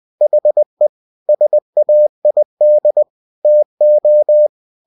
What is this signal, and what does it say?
Morse code, heard as a keyed tone, about 20 wpm, 600 Hz. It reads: HE SAID TO
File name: cw-books / winnie-the-pooh / winnie-20wpm-page-0036